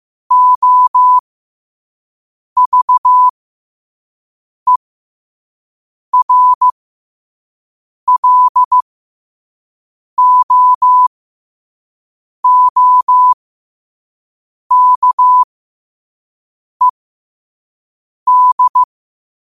Morse code, heard as a keyed tone, about 15 wpm, spaced as for 5 wpm, 1000 Hz. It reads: OVERLOOKED
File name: qotdfiles / 5wpm